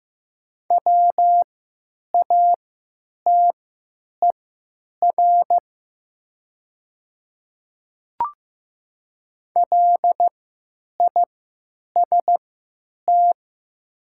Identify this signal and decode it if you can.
Morse code, heard as a keyed tone, about 15 wpm, spaced as for 9 wpm, 700 Hz. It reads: WATER LIST
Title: 3x Character Spacing - Top 500 Words - 15wpm